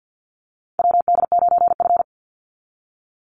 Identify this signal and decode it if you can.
Morse code, heard as a keyed tone, about 48 wpm, 700 Hz. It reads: WD9F
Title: NA SOTA Chaser Callsigns 50wpm